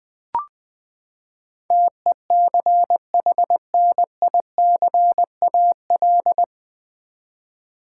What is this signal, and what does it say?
Morse code, heard as a keyed tone, about 20 wpm, 700 Hz. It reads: TECHNICAL
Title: Sentences from Top 1100 Words - New Word Drill v2 - 20wpm